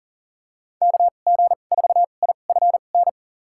Morse code, heard as a keyed tone, about 40 wpm, spaced as for 29 wpm, 700 Hz. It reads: KG4IFN
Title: USA Call Signs - 6 Characters - 2x Character Spacing - 40wpm